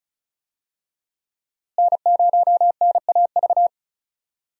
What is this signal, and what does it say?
Morse code, heard as a keyed tone, about 36 wpm, 700 Hz. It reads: N0NAV
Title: USA Call Signs - 5 Characters - Repeated 3x - 35wpm